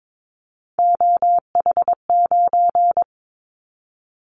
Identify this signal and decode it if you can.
Morse code, heard as a keyed tone, about 22 wpm, 700 Hz. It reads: OH9
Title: Callsign Prefixes for Top 50 DXCC - 22wpm